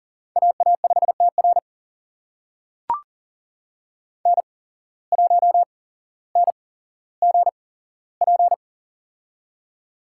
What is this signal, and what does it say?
Morse code, heard as a keyed tone, about 42 wpm, spaced as for 17 wpm, 700 Hz. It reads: AA5TR N1NGP
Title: USA Call Signs - 8x Character Spacing - 40wpm